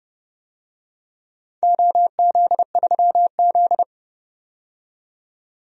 Morse code, heard as a keyed tone, about 30 wpm, 700 Hz. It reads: OZ3Z
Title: DXCC-100 Call Signs - 4 Characters - 30wpm